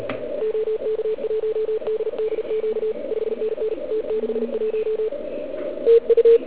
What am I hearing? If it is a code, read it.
Morse code, heard as a keyed tone, about 36 wpm, 440 Hz. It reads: OK1DDQ 5NN T61 TU